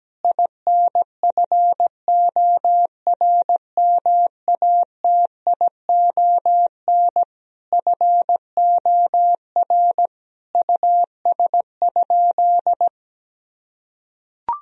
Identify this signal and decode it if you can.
Morse code, heard as a keyed tone, about 17 wpm, 700 Hz. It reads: INFORMATION FOR US?
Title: Sentences from Top 1000 Words - Review Part 3 - 17wpm